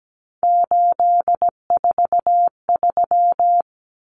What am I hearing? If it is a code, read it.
Morse code, heard as a keyed tone, about 17 wpm, 700 Hz. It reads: 843
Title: Common Cut Numbers - 3-Digit 17wpm